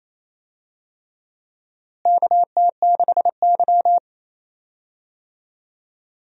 Morse code, heard as a keyed tone, about 28 wpm, 700 Hz. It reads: KT6Y
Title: USA Call Signs - 4 Characters - Repeated 3x - 28wpm